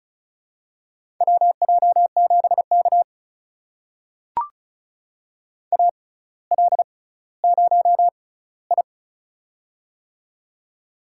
Morse code, heard as a keyed tone, about 36 wpm, spaced as for 18 wpm, 700 Hz. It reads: WJ7K AL0I
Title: USA Call Signs - 4 Characters - 6x Character Spacing - 35wpm